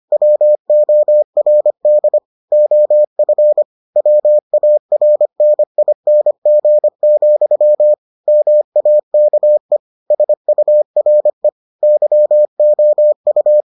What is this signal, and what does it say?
Morse code, heard as a keyed tone, about 25 wpm, 600 Hz. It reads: WORD OF WARNING, MAKE SURE YOU